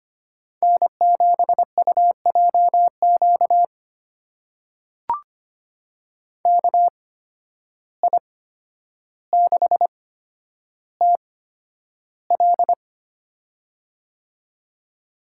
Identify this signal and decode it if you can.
Morse code, heard as a keyed tone, about 25 wpm, spaced as for 11 wpm, 700 Hz. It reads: N7UJQ KI6TL
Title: USA Call Signs - 5 Characters - 8x Character Spacing - 25wpm